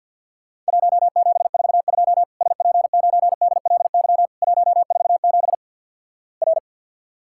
Morse code, heard as a keyed tone, about 53 wpm, 700 Hz. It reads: 1742 SP9DLY 147 R
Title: Worked All Europe DX Contest - QTC Exchange with Repeats - 50wpm